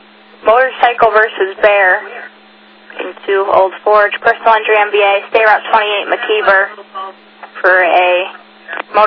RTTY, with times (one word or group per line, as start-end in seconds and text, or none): none